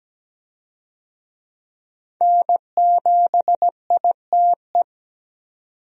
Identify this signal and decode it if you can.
Morse code, heard as a keyed tone, about 17 wpm, 700 Hz. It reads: N7ITE
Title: USA Call Signs - 5 Characters - 17wpm